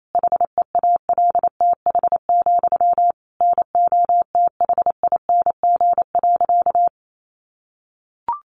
Morse code, heard as a keyed tone, about 28 wpm, 700 Hz. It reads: HEALTH, NOTHING.
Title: Sentences from Top 900 Words 28wpm